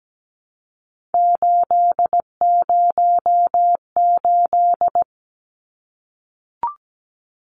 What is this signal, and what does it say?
Morse code, heard as a keyed tone, about 17 wpm, 700 Hz. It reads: 808
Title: Numbers - 3-Digit 17wpm